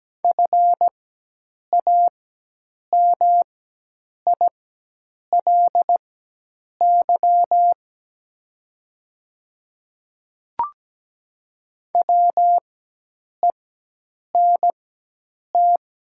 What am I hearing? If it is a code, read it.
Morse code, heard as a keyed tone, about 17 wpm, spaced as for 8 wpm, 700 Hz. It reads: FAMILY WENT
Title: 4x Character Spacing - Top 500 Words - 17wpm